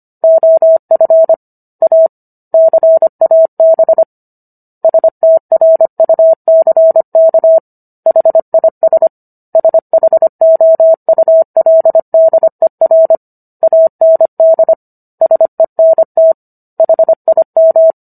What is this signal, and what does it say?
Morse code, heard as a keyed tone, about 25 wpm, 650 Hz. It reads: OF A CAB STRUCK HIS SHOULDER AND SENT HIM